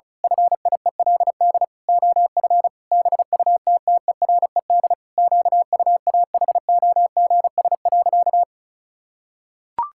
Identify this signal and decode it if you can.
Morse code, heard as a keyed tone, about 36 wpm, 700 Hz. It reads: FIELD OF BUTTERED QUAHOGS.
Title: Pangram Challenge v2 - 35wpm